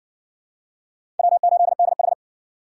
Morse code, heard as a keyed tone, about 61 wpm, 700 Hz. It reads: W8DH